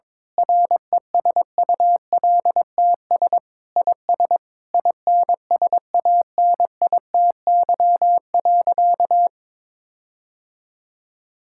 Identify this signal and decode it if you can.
Morse code, heard as a keyed tone, about 22 wpm, 700 Hz. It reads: RESULTS IS INSANITY.
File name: Sentences with 9-Word Phrases - 22wpm